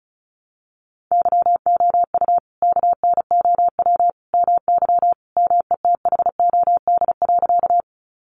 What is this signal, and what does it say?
Morse code, heard as a keyed tone, about 34 wpm, 700 Hz. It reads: YOU KNOW MY METHOD.
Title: Sentences from Top 600 Words 35wpm